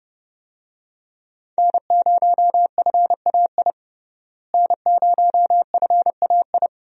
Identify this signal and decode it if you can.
Morse code, heard as a keyed tone, about 30 wpm, 700 Hz. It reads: N0FAI N0FAI
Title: USA Call Signs - 5 Characters - Repeated 3x - 30wpm